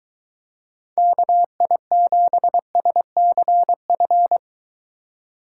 Morse code, heard as a keyed tone, about 22 wpm, 700 Hz. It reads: KI7SCF